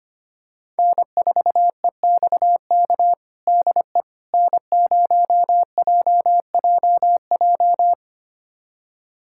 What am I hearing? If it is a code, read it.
Morse code, heard as a keyed tone, about 25 wpm, 700 Hz. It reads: N4EXK DE N0JJJ
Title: USA Callsign DE Callsign - 5 Characters - 25wpm